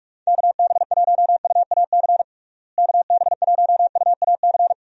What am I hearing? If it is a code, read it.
Morse code, heard as a keyed tone, about 42 wpm, 700 Hz. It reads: KB1UAC KB1UAC
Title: USA Call Signs - 6 Characters - Repeated 3x - 45wpm